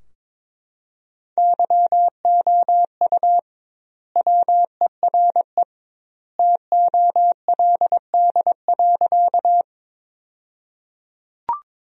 Morse code, heard as a keyed tone, about 22 wpm, 700 Hz. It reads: YOU WERE TOLD.
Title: 2x Word-Spacing - Sentences from Top 400 Words - 22wpm